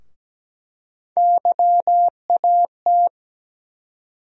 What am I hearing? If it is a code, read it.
Morse code, heard as a keyed tone, about 17 wpm, 700 Hz. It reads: YAT
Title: USA Call Sign Postfixes - 3 Letters 17wpm